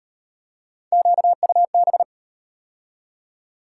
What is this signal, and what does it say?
Morse code, heard as a keyed tone, about 37 wpm, 700 Hz. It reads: QUB